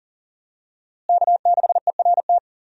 Morse code, heard as a keyed tone, about 40 wpm, 700 Hz. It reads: K6ERT
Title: USA Call Signs - 5 Characters - Repeated 3x - 40wpm